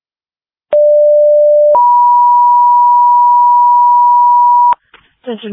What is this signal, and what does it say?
Two-tone page, heard as a one-second tone, then a three-second tone, about 590 then 970 Hz, with speech. It is heard from 0.7 to 4.7 s.